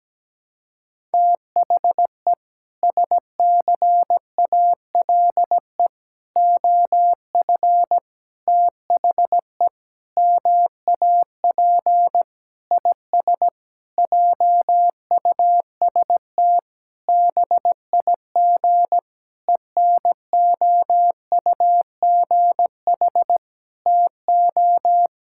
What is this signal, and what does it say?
Morse code, heard as a keyed tone, about 17 wpm, 700 Hz. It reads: THE SCALE OF THE MAP IS JUST BIG ENOUGH TO
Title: Sentences from Top 1000 Words - Review Part 5 - 17wpm